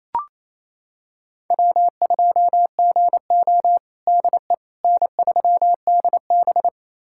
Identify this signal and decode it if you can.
Morse code, heard as a keyed tone, about 28 wpm, 700 Hz. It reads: W2GO DE N3DB
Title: USA Callsign DE Callsign - 4 Characters - 28wpm